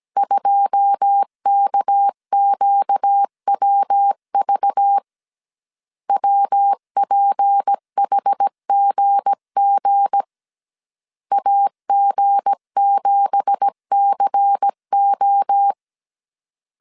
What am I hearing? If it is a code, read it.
Morse code, heard as a keyed tone, about 17 wpm, 800 Hz. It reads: 2KQWV WPHGG AG7CO
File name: CW / DET / 1 9 1500C S C 16wpm 2909